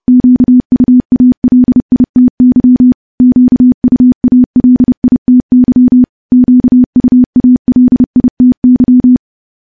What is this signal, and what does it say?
Morse code, heard as a keyed tone, about 30 wpm, 250 Hz. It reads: QUALITY QUALITY QUALITY
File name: morse / morse-training-30WPM